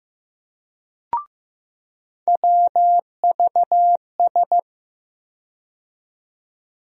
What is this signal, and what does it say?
Morse code, heard as a keyed tone, about 15 wpm, 700 Hz. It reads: WVS
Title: USA Call Sign Postfixes - 3 Letters 15wpm